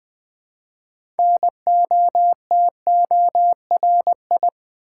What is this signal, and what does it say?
Morse code, heard as a keyed tone, about 20 wpm, 700 Hz. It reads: NOTORI
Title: Build A Word - 3 Letters At A Time - 9-Letter Words - 20wpm